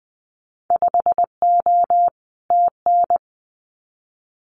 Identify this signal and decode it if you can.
Morse code, heard as a keyed tone, about 21 wpm, 700 Hz. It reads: 5O TN
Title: Winter Field Day Exchanges Encore 20wpm